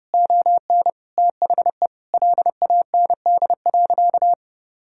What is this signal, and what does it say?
Morse code, heard as a keyed tone, about 30 wpm, 700 Hz. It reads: ON THE LAND.
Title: Sentences from Top 1000 Words - Review Part 5 - 30wpm